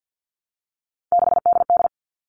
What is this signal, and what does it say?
Morse code, heard as a keyed tone, about 50 wpm, 700 Hz. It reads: 6DD
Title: All Cut Numbers - 3-Digit 50wpm